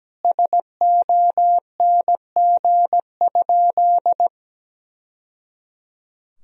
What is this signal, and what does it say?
Morse code, heard as a keyed tone, about 17 wpm, 700 Hz. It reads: SONG?
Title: Sentences from Top 1000 Words - Review Part 2 - 17wpm